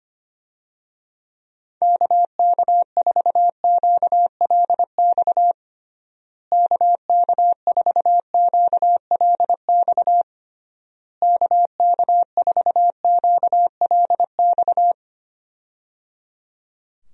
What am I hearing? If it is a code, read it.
Morse code, heard as a keyed tone, about 25 wpm, 700 Hz. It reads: KK4QLX KK4QLX KK4QLX
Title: USA Call Signs - 6 Characters - Repeated 3x - 25wpm